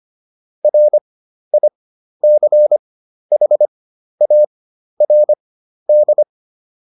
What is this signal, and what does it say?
Morse code, heard as a keyed tone, about 25 wpm, spaced as for 12 wpm, 600 Hz. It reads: RICHARD
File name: QSO 203 12